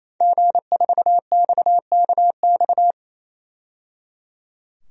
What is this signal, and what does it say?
Morse code, heard as a keyed tone, about 28 wpm, 700 Hz. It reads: G4XKX
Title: DXCC-100 Call Signs - 5 Characters - 28wpm